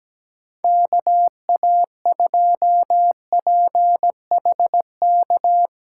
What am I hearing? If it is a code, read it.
Morse code, heard as a keyed tone, about 17 wpm, 700 Hz. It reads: KA2PHK
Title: USA Call Signs - 8x Character Spacing - 17wpm